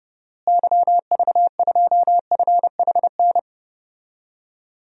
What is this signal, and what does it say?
Morse code, heard as a keyed tone, about 30 wpm, 700 Hz. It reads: YV2FHN
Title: DXCC-100 Call Signs - 6 Characters - Repeated 3x - 30wpm